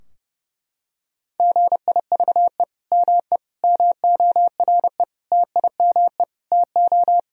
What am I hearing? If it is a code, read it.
Morse code, heard as a keyed tone, about 30 wpm, 700 Hz. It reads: GIVE ME MORE TIME TO
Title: Sentences from Top 1000 Words - Review Part 3 - 30wpm